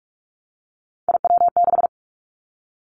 Morse code, heard as a keyed tone, about 46 wpm, 700 Hz. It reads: IW6
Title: Three Challenging Letter-Number Combinations 45wpm